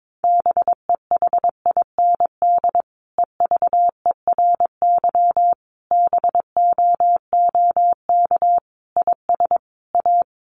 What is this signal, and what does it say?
Morse code, heard as a keyed tone, about 22 wpm, 700 Hz. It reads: BEHIND EVERY BOOK IS A
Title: Sentences from Top 500 Words 22wpm